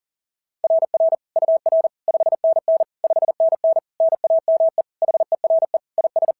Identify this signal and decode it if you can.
Morse code, heard as a keyed tone, about 40 wpm, 660 Hz. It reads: RR UR 5NN 5NN NAME HERE IS
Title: SOTA QSO Practice 40wpm